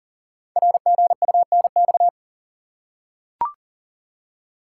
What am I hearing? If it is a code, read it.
Morse code, heard as a keyed tone, about 39 wpm, 700 Hz. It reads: RGUNX
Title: Code Groups - 5 Letters - 40wpm